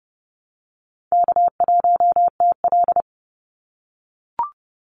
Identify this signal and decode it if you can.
Morse code, heard as a keyed tone, about 30 wpm, 700 Hz. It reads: K1TL
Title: Call Signs - 30wpm